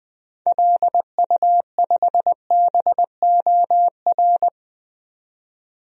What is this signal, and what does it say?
Morse code, heard as a keyed tone, about 20 wpm, 700 Hz. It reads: LU5BOR
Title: DXCC-100 Call Signs - 6 Characters - 20wpm